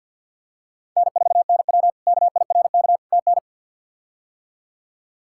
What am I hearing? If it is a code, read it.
Morse code, heard as a keyed tone, about 52 wpm, 700 Hz. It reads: N4NW KIRK TN
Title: K1USN SST QSO Practice - 50wpm